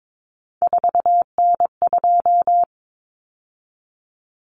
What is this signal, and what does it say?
Morse code, heard as a keyed tone, about 22 wpm, 700 Hz. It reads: 4N2
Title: Common Cut Numbers - 3-Digit 22wpm